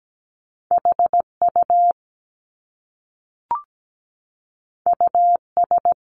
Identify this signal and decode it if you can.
Morse code, heard as a keyed tone, about 17 wpm, 700 Hz. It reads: HU US
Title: Top Two Letter Combinations in Words 17wpm